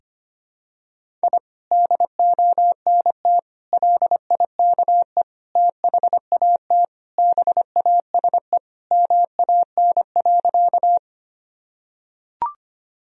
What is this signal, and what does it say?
Morse code, heard as a keyed tone, about 25 wpm, 700 Hz. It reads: I DONT LIKE THAT BASE MAN.
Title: Sentences from Top 1000 Words - Review Part 1 - 25wpm